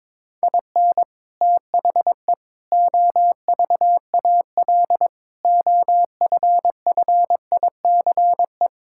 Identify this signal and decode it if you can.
Morse code, heard as a keyed tone, about 22 wpm, 700 Hz. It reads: IN THE OVAL OFFICE
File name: practice / 22wpm / Sets of 8 Words - 22wpm